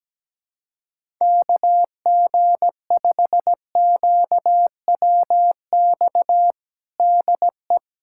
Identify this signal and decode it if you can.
Morse code, heard as a keyed tone, about 17 wpm, 700 Hz. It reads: KG5QWX DE
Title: USA Callsign DE Callsign - 6 Characters - 17wpm